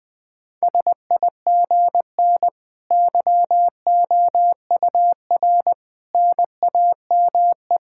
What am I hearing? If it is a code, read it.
Morse code, heard as a keyed tone, about 20 wpm, 700 Hz. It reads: SIGN YOUR NAME
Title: Sentences with 5-Word Phrases - Spoken Beginning - 20wpm